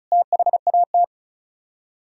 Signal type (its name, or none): Morse code